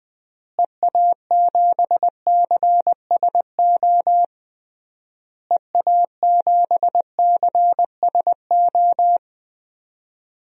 Morse code, heard as a keyed tone, about 20 wpm, 700 Hz. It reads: EA7CSO EA7CSO
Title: DXCC-100 Call Signs - 6 Characters - Repeated 3x - 20wpm